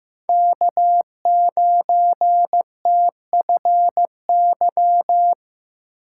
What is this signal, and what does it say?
Morse code, heard as a keyed tone, about 18 wpm, 700 Hz. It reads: K9TFY
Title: Speed Racing - USA 1x3 Call Signs 15wpm